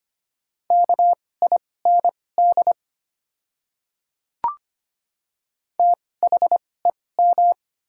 Morse code, heard as a keyed tone, about 25 wpm, spaced as for 18 wpm, 700 Hz. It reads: KIND THEM
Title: 2x Character Spacing - Top 200 Words - 25wpm